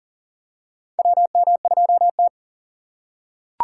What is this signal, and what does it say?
Morse code, heard as a keyed tone, about 40 wpm, 700 Hz. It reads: WM2T
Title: USA Call Signs - 4 Characters - 2x Character Spacing - 40wpm